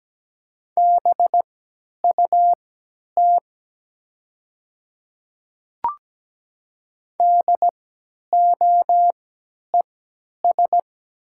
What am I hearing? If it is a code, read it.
Morse code, heard as a keyed tone, about 17 wpm, spaced as for 10 wpm, 700 Hz. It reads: BUT DOES
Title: 3x Character Spacing - Top 500 Words - 17wpm